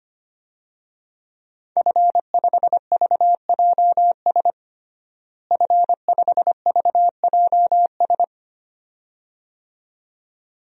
Morse code, heard as a keyed tone, about 25 wpm, 700 Hz. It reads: F5VJS F5VJS
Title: DXCC-100 Call Signs - 5 Characters - Repeated 2x - 25wpm